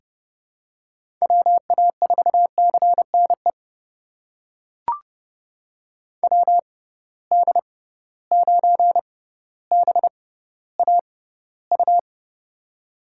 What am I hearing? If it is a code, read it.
Morse code, heard as a keyed tone, about 30 wpm, spaced as for 15 wpm, 700 Hz. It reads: WA4CNE WD9BAU
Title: USA Call Signs - 6 Characters - 6x Character Spacing - 30wpm